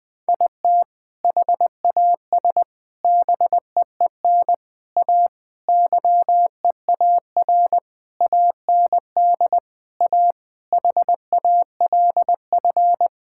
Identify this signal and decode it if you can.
Morse code, heard as a keyed tone, about 20 wpm, 700 Hz. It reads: IT HAS BEEN A YEAR AND A HALF
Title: Sentences with 4-Word Phrases - Spoken Beginning - 20wpm